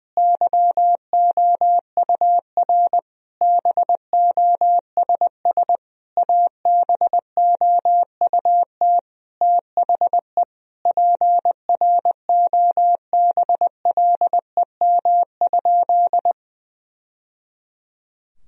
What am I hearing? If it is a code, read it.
Morse code, heard as a keyed tone, about 20 wpm, 700 Hz. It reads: YOUR BOSS ABOUT THE PROBLEM?
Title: Sentences from Top 1000 Words - Review Part 4 - 20wpm